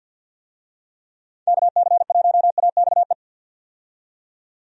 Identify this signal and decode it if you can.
Morse code, heard as a keyed tone, about 48 wpm, 700 Hz. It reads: KC1AXE
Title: USA Call Signs - 6 Characters - 50wpm